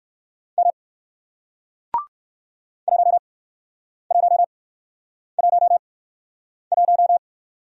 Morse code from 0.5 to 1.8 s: M

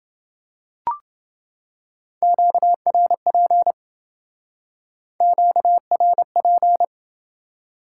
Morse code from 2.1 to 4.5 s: QRP

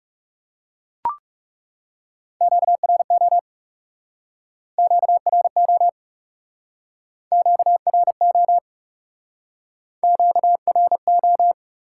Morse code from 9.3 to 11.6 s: QRO

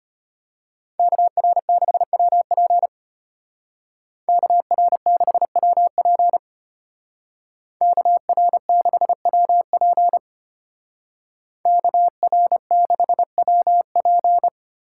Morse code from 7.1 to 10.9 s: KR6WP

